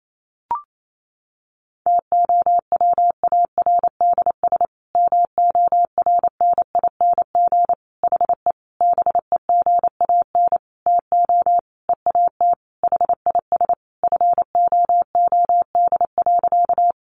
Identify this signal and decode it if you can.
Morse code, heard as a keyed tone, about 28 wpm, 700 Hz. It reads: TOWARDS MORNING HE BEGAN TO EAT HIS FOOD.
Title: Sentences from Top 1100 Words 28wpm